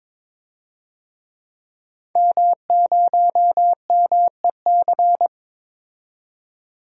Morse code, heard as a keyed tone, about 21 wpm, 700 Hz. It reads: M0MEC
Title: DXCC-100 Call Signs - 5 Characters - 22wpm